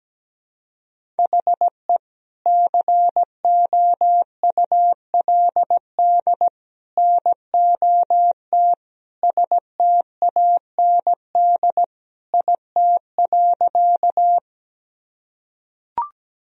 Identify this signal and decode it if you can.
Morse code, heard as a keyed tone, about 17 wpm, 700 Hz. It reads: HE COULD NOT STAND IT.
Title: Sentences from Top 1000 Words - Review Part 1 - 17wpm